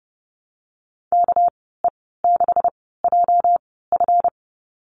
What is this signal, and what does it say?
Morse code, heard as a keyed tone, about 30 wpm, spaced as for 17 wpm, 700 Hz. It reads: KE6JF
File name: Lesson 37g - Letter x - Call Signs 30wpm